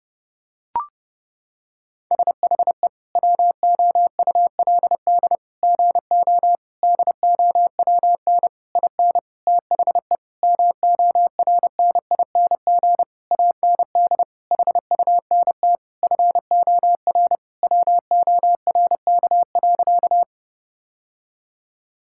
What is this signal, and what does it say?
Morse code, heard as a keyed tone, about 32 wpm, 700 Hz. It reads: SHE WOULD GO DOWN IN THE MORNING AND HUNT FOR WORK.
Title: Sentences from Top 1600 Words 30wpm